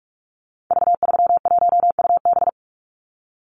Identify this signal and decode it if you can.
Morse code, heard as a keyed tone, about 44 wpm, 700 Hz. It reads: V31UB
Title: Call Signs Repeated 2x - 45wpm